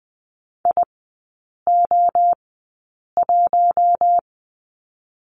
Morse code, heard as a keyed tone, about 20 wpm, 700 Hz. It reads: I O 1
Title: 047 Lesson 9e - Number 4 - Three Characters 20wpm